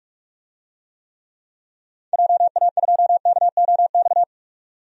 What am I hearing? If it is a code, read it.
Morse code, heard as a keyed tone, about 45 wpm, 700 Hz. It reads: JA2KOX
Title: DXCC-100 Call Signs - 6 Characters - 45wpm